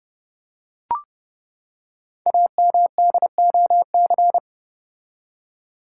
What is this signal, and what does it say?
Morse code, heard as a keyed tone, about 28 wpm, 700 Hz. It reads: AMDOC
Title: Code Groups - 5 Letters - 30wpm